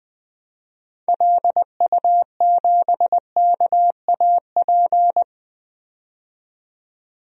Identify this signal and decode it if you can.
Morse code, heard as a keyed tone, about 20 wpm, 700 Hz. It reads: LU7KAP
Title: DXCC-100 Call Signs - 6 Characters - 20wpm